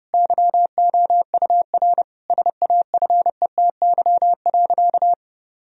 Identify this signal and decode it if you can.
Morse code, heard as a keyed tone, about 30 wpm, 700 Hz. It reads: YOUR SAFETY.
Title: Sentences from Top 1000 Words - Review Part 4 - 30wpm